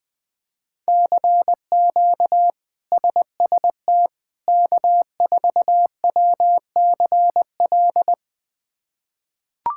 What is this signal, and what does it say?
Morse code, heard as a keyed tone, about 20 wpm, 700 Hz. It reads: CQ SST K4WCL
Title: K1USN SST QSO Practice - 20wpm